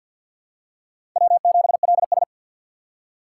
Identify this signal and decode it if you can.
Morse code, heard as a keyed tone, about 52 wpm, 700 Hz. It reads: W7LS
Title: USA Call Signs - 4 Characters - 2x Character Spacing - 50wpm